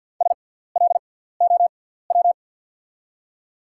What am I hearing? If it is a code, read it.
Morse code, heard as a keyed tone, about 50 wpm, spaced as for 17 wpm, 690 Hz. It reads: SLOW
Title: 6x Character Spacing - Top 500 Words - 50wpm